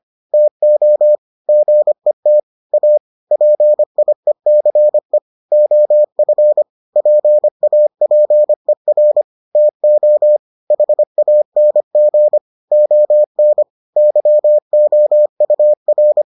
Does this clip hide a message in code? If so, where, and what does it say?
Yes: Morse code from 0.3 to 16.2 s, heard as a keyed tone, about 25 wpm, 600 Hz. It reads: TO GET A PIECE OF PAPER TO HANG ON YOUR